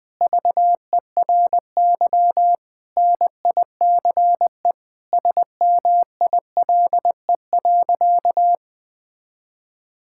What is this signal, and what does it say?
Morse code, heard as a keyed tone, about 21 wpm, 700 Hz. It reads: VERY NICE SMILE.
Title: Sentences with 2-Word Phrases - Spoken Beginning - 20wpm